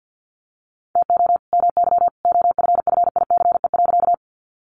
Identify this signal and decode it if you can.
Morse code, heard as a keyed tone, about 50 wpm, 700 Hz. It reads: TO MY OFFICE.
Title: Sentences from Top 1000 Words 50wpm